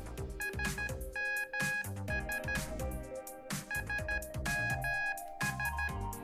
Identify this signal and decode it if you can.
Morse code, heard as a keyed tone, about 13 wpm, 1650 Hz. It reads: SMS SMS